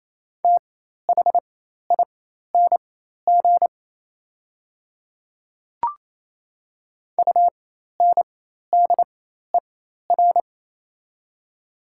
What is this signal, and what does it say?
Morse code, heard as a keyed tone, about 27 wpm, spaced as for 13 wpm, 690 Hz. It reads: THING UNDER
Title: 4x Character Spacing - Top 200 Words - 28wpm